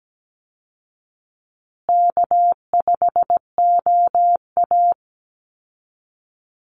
Morse code, heard as a keyed tone, about 17 wpm, 700 Hz. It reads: K5OA
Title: Easy Call Signs Repeated 3x - 17wpm